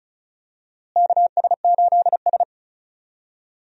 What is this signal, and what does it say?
Morse code, heard as a keyed tone, about 36 wpm, 700 Hz. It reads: KS8S
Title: USA Call Signs - 4 Characters - Repeated 3x - 35wpm